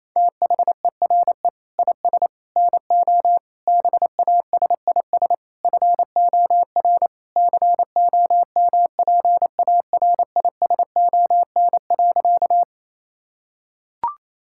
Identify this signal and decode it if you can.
Morse code, heard as a keyed tone, about 28 wpm, 700 Hz. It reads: THERE IS NO BASIS FOR COMPARISON.
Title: Sentences from Top 1600 Words 28wpm